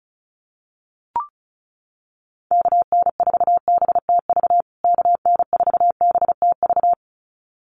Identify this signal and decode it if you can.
Morse code, heard as a keyed tone, about 34 wpm, 700 Hz. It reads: KN4BTV KN4BTV
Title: USA Call Signs Repeated 2x 35wpm